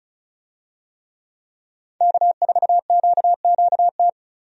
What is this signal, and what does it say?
Morse code, heard as a keyed tone, about 34 wpm, 700 Hz. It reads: K4QQT
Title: USA Call Signs - 5 Characters - 35wpm